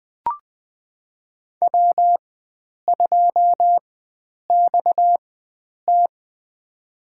Morse code, heard as a keyed tone, about 21 wpm, spaced as for 9 wpm, 700 Hz. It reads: W2XT
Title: USA Call Signs - 4x Character Spacing - 20wpm